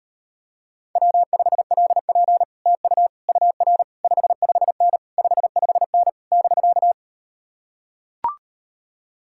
Morse code, heard as a keyed tone, about 38 wpm, 700 Hz. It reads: W5LP TU UR 55N 55N <BK>